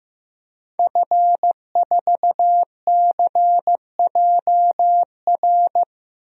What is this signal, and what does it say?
Morse code, heard as a keyed tone, about 15 wpm, 700 Hz. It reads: F4CJR
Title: DXCC-100 Call Signs - 5 Characters - 15wpm